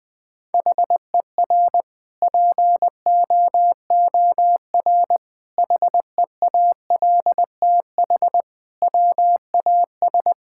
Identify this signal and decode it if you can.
Morse code, heard as a keyed tone, about 21 wpm, 700 Hz. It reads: HER POOR HEALTH WAS
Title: Sentences from Top 1000 Words - Review Part 6 - 20wpm